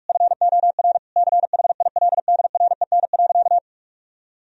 Morse code, heard as a keyed tone, about 41 wpm, 700 Hz. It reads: FOR CHILDREN.
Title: Sentences from Top 1000 Words - Review Part 4 - 45wpm